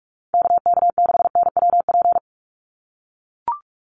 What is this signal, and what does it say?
Morse code, heard as a keyed tone, about 44 wpm, 700 Hz. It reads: KK6NWP
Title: USA Call Signs 45wpm